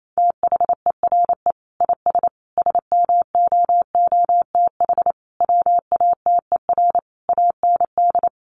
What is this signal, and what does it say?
Morse code, heard as a keyed tone, about 28 wpm, 700 Hz. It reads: THERE IS SMOOTH WATER AND
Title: Sentences from Top 1400 Words 28wpm